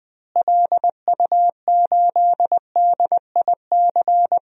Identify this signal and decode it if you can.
Morse code, heard as a keyed tone, about 21 wpm, 700 Hz. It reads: LU8DIC
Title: DXCC-100 Call Signs - 6 Characters - Repeated 2x - 20wpm